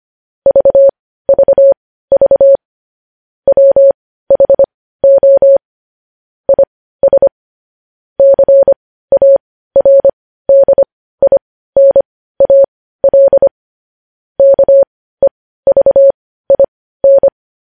Morse code, heard as a keyed tone, about 25 wpm, spaced as for 15 wpm, 550 Hz. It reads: VVV WHO IS CARDINAL KEVIN
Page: News headlines in morse code at 15 wpm